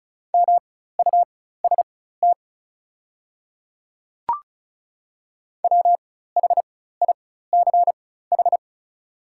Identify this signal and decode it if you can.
Morse code, heard as a keyed tone, about 36 wpm, spaced as for 16 wpm, 700 Hz. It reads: MUST WHICH
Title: 4x Character Spacing - Top 200 Words - 35wpm